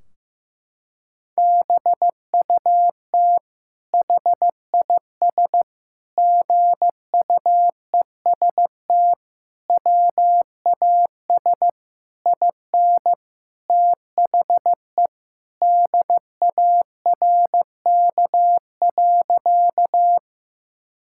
Morse code, heard as a keyed tone, about 16 wpm, 700 Hz. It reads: BUT HIS GUEST WAS IN THE DARK.
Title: Sentences from Top 1600 Words 15wpm